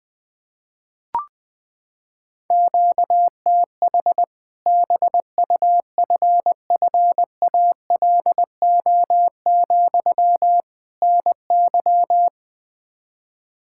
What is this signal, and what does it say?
Morse code, heard as a keyed tone, about 20 wpm, 700 Hz. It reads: QTH BUFFALO, NY
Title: Practice QSO - Exchange 1 - Beginning - 20wpm